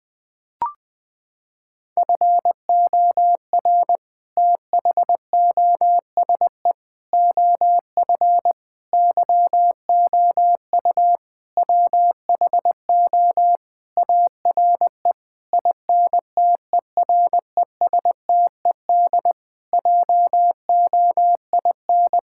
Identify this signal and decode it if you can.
Morse code, heard as a keyed tone, about 20 wpm, 700 Hz. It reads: FOR THOSE OF YOU WHO ARE INTERESTED JOIN
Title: Sentences with 6-Word Phrases - 20wpm